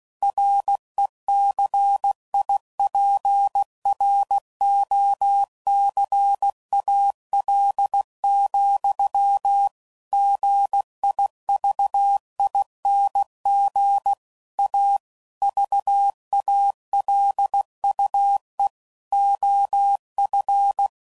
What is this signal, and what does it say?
Morse code, heard as a keyed tone, about 16 wpm, 800 Hz. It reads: RECIPROCAL, GIVING A VALUE OF
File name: CW Code Groups @15WPM / Technical Subjects / XC